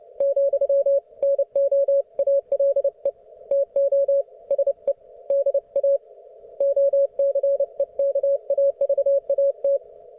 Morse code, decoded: , NO ALE TO SE DA OCEKAVAT

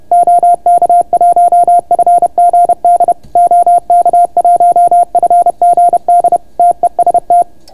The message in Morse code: OK1FGD OK1FGD TEST

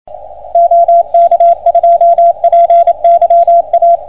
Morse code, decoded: OK2PYA

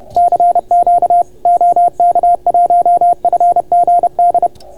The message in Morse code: CQ OK1FGD

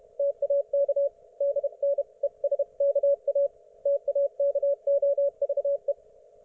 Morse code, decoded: TAK DNESKA TAKOVE

